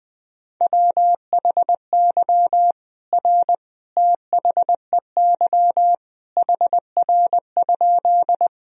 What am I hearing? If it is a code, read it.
Morse code, heard as a keyed tone, about 20 wpm, 700 Hz. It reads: WHY R THEY HR?
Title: Sentences from Top 200 Words and Abbreviations 20wpm